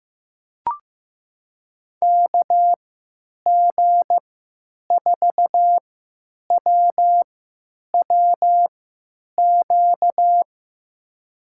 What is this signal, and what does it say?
Morse code, heard as a keyed tone, about 15 wpm, spaced as for 9 wpm, 700 Hz. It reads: KG4WWQ
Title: USA Call Signs - 6 Characters - 3x Character Spacing - 15wpm